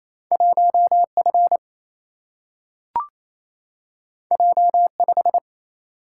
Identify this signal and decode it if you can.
Morse code, heard as a keyed tone, about 28 wpm, 700 Hz. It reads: 1 F J 5
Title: Single-Letter Single-Number 28wpm